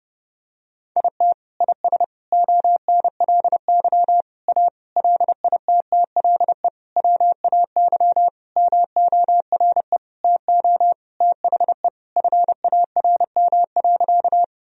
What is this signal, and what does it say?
Morse code, heard as a keyed tone, about 30 wpm, 700 Hz. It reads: IT IS ONLY A LITTLE WAY MORE TO THE FARM.
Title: Sentences from Top 1000 Words - Review Part 2 - 30wpm